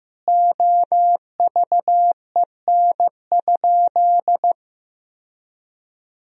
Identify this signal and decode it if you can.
Morse code, heard as a keyed tone, about 15 wpm, 700 Hz. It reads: OVEN?
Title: Sentences from Top 1000 Words - Review Part 4 - 15wpm